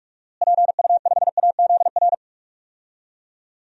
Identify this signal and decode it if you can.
Morse code, heard as a keyed tone, about 42 wpm, 700 Hz. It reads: PU5AZR
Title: DXCC-100 Call Signs - 6 Characters - Repeated 3x - 45wpm